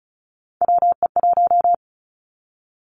Morse code, heard as a keyed tone, about 36 wpm, 700 Hz. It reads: WE1
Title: Three Challenging Letter-Number Combinations 35wpm